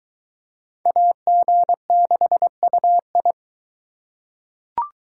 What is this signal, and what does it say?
Morse code, heard as a keyed tone, about 22 wpm, 700 Hz. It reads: AG6UI